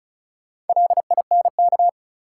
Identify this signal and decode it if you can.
Morse code, heard as a keyed tone, about 34 wpm, 700 Hz. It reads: LINK
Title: Sentences from Top 1000 Words - Review Part 7 - New Word Drill - 35wpm